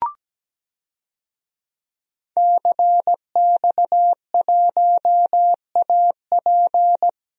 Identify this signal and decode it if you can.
Morse code, heard as a keyed tone, about 17 wpm, 700 Hz. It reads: CX1AP
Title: DXCC-100 Call Signs - 5 Characters - Repeated 2x - 17wpm